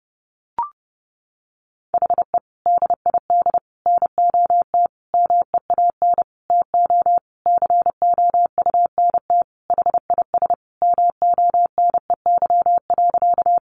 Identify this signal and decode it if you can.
Morse code, heard as a keyed tone, about 28 wpm, 700 Hz. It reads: HE DID NOT MEAN TO COUNT HIS MONEY.
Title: Sentences from Top 1500 Words 30wpm